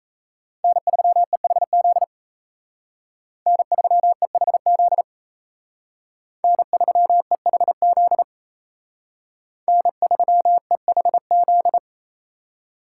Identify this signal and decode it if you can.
Morse code, heard as a keyed tone, about 36 wpm, 700 Hz. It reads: N3EHZ N3EHZ N3EHZ N3EHZ